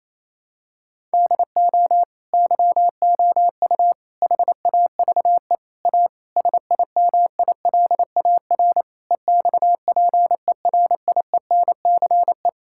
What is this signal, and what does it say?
Morse code, heard as a keyed tone, about 28 wpm, 700 Hz. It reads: DO YOU HAVE A SIMILAR EXPERIENCE
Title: Sentences from Top 1000 Words - Review Part 5 - 28wpm